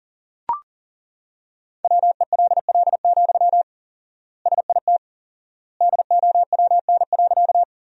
Morse code, heard as a keyed tone, about 40 wpm, 700 Hz. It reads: WELL, SIT DOWN.
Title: 4x Word-Spacing - Sentences from Top 500 Words - 40wpm